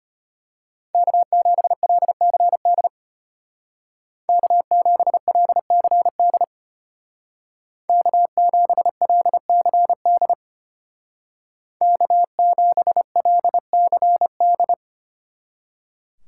Morse code, heard as a keyed tone, about 30 wpm, 700 Hz. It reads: K7LCD K7LCD K7LCD K7LCD
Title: Speed Racing - USA 1x3 Call Signs 25wpm